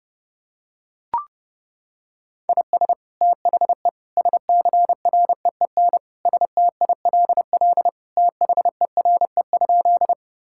Morse code, heard as a keyed tone, about 30 wpm, 700 Hz. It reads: IS THE SCREEN STILL THERE?
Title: Sentences from Top 1000 Words - Review Part 6 - 30wpm